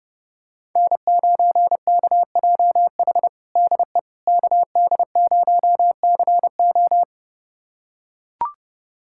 Morse code, heard as a keyed tone, about 32 wpm, 700 Hz. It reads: N9KJH DE KD0CO
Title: USA Callsign DE Callsign - 5 Characters - 30wpm